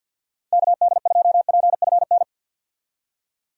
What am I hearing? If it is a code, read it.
Morse code, heard as a keyed tone, about 52 wpm, 700 Hz. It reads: KD2PFN
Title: USA Call Signs - 6 Characters - 50wpm